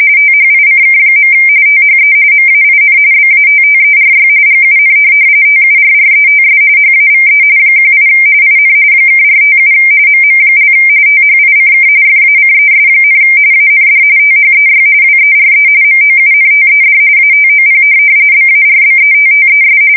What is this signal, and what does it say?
RTTY, about 45.45 baud. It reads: THICK. THE MAIN BATTERY GUN TURRETS HAVE 18 IN (457 MM) THICK FACES, AND THEY WERE MOUNTED ATOP BARBETTES